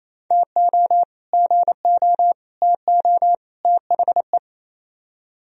Morse code, heard as a keyed tone, about 36 wpm, 700 Hz. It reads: TO GO TO THE